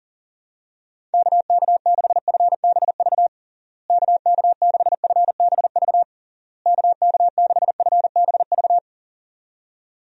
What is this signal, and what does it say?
Morse code, heard as a keyed tone, about 37 wpm, 700 Hz. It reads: KK6FBV KK6FBV KK6FBV
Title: USA Call Signs - 6 Characters - Repeated 3x - 40wpm